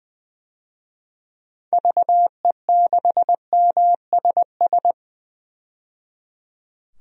Morse code, heard as a keyed tone, about 20 wpm, 700 Hz. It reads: VE6MSS
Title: DXCC-100 Call Signs - 6 Characters - 20wpm